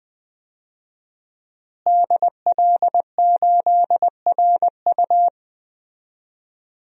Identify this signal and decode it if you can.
Morse code, heard as a keyed tone, about 20 wpm, 700 Hz. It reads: DL8RU
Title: DXCC-100 Call Signs - 5 Characters - Repeated 2x - 20wpm